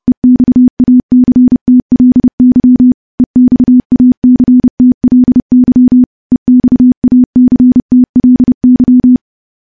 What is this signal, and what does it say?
Morse code, heard as a keyed tone, about 32 wpm, 250 Hz. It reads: EXACTLY EXACTLY EXACTLY
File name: morse / morse-training-30WPM